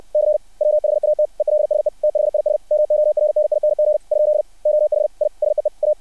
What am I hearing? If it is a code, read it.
Morse code, heard as a keyed tone, about 60 wpm, 600 Hz. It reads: C WHAT ELSE THIS MORNING ? OH I SEE A
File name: archived / CRSnet